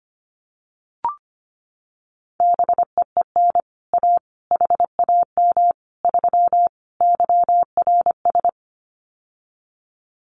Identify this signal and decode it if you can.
Morse code, heard as a keyed tone, about 25 wpm, 700 Hz. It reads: BEEN A HAM 3 YRS
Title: Common QSO Phrases Encore 25wpm